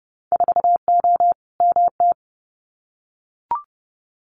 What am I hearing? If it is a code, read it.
Morse code, heard as a keyed tone, about 30 wpm, 700 Hz. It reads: 4O MT
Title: Winter Field Day Exchanges Encore 30wpm